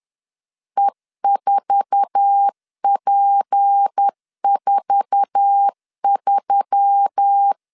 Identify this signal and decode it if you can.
Morse code, heard as a keyed tone, about 10 wpm, 800 Hz. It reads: E4P43
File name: LEZIONI / 8 YJ45 / 8 YJ4 10wpm 2633